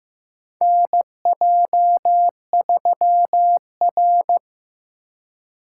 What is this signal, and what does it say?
Morse code, heard as a keyed tone, about 15 wpm, 700 Hz. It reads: NJ3R
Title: USA Call Signs - 4 Characters - 4x Character Spacing - 15wpm